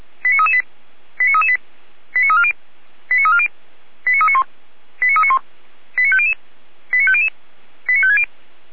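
Five-tone selective call (ZVEI), heard as quick runs of five tones, tones of about 70 ms. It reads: 89338 89338 89340 89340 89391 89391 89500 89500 89560
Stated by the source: ZVEI five-tone selective call 89336 received on an SDR.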